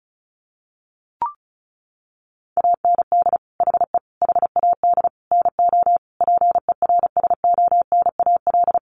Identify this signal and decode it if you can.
Morse code, heard as a keyed tone, about 34 wpm, 700 Hz. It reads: AND HE HAD NO PERSONAL
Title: Sentences from Top 1400 Words 35wpm